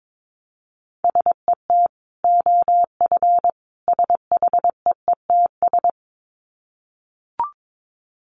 Morse code, heard as a keyed tone, about 22 wpm, 700 Hz. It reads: SET OF SHEETS
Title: Dit Challenge 22wpm